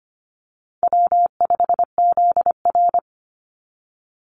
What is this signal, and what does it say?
Morse code, heard as a keyed tone, about 25 wpm, 700 Hz. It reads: W5ZR
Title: Easy Call Signs Repeated 3x - 25wpm